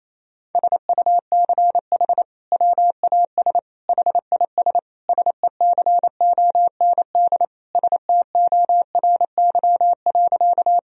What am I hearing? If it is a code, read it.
Morse code, heard as a keyed tone, about 28 wpm, 700 Hz. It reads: SUCH WAS HIS SECOND STORY.
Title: Sentences from Top 1000 Words - Review Part 1 - 28wpm